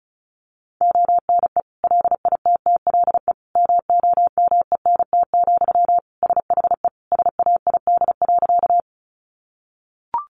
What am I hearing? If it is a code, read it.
Morse code, heard as a keyed tone, about 34 wpm, 700 Hz. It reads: ONE LITTLE MOMENT, SHE SAID.
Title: Sentences from Top 1000 Words 35wpm